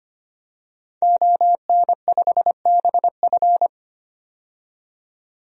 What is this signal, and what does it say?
Morse code, heard as a keyed tone, about 25 wpm, 700 Hz. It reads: ON5BF
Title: DXCC-100 Call Signs - 5 Characters - 25wpm